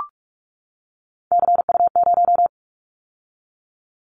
Morse code, heard as a keyed tone, about 44 wpm, 700 Hz. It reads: CU0
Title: Callsign Prefixes for Top 100 DXCC - 45wpm